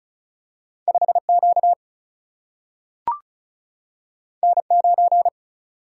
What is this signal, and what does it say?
Morse code, heard as a keyed tone, about 36 wpm, 700 Hz. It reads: 5Q N9
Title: Single-Letter Single-Number 35wpm